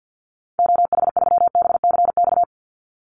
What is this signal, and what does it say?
Morse code, heard as a keyed tone, about 49 wpm, 700 Hz. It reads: OH3BCX
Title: Call Signs Repeated 3x - 50wpm